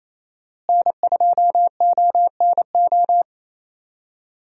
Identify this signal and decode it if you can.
Morse code, heard as a keyed tone, about 28 wpm, 700 Hz. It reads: N2ONO
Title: USA Call Signs - 5 Characters - Repeated 3x - 28wpm